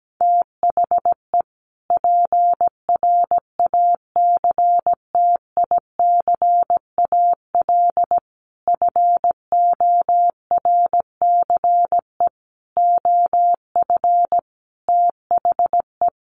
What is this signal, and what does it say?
Morse code, heard as a keyed tone, about 17 wpm, 700 Hz. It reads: THE PRACTICAL FORCE OF THE
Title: Sentences from Top 1400 Words 17wpm